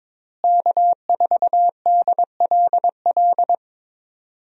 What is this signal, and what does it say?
Morse code, heard as a keyed tone, about 22 wpm, 700 Hz. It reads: K4DLL
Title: USA Call Signs - 5 Characters - Repeated 2x - 22wpm